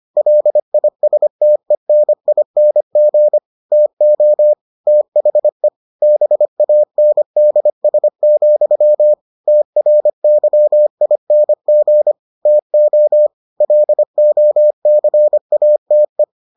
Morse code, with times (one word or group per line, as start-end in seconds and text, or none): 0.2-3.4 LISTENING
3.7-4.5 TO
4.9-5.7 THE
6.0-9.1 BANDS,
9.5-12.1 TRYING
12.5-13.3 TO
13.6-16.2 LOCATE